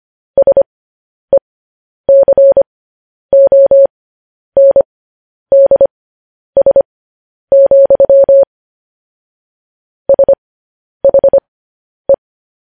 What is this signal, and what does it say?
Morse code, heard as a keyed tone, about 25 wpm, spaced as for 10 wpm, 550 Hz. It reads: SECONDS, SHE
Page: news-headlines-in-morse-code-at-10-wpm